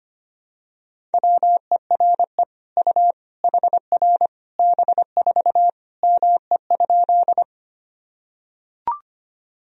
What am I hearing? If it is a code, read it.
Morse code, heard as a keyed tone, about 25 wpm, 700 Hz. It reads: WERE U HR B4 ME?
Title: Sentences from Top 200 Words and Abbreviations 25wpm